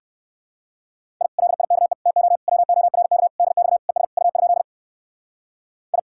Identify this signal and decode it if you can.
Morse code, heard as a keyed tone, about 65 wpm, 700 Hz. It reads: I LIKE TO WORK DX ES U? I